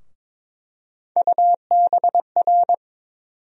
Morse code, heard as a keyed tone, about 21 wpm, 700 Hz. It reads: UBR
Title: USA Call Sign Postfixes - 3 Letters 22wpm